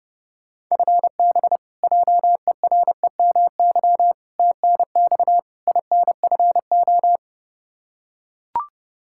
Morse code, heard as a keyed tone, about 32 wpm, 700 Hz. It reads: FB JEREMY TNX INFO
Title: Practice QSO - Exchange 3 - More Info - 30wpm